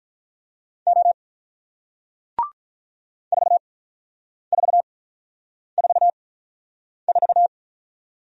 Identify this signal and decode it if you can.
Morse code, heard as a keyed tone, about 46 wpm, spaced as for 9 wpm, 700 Hz. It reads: O 4444